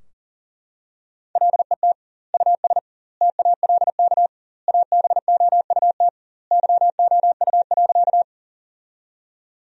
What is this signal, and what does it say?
Morse code, heard as a keyed tone, about 40 wpm, 700 Hz. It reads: LET US TALK ABOUT YOU.
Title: 2x Word-Spacing - Sentences from Top 400 Words - 40wpm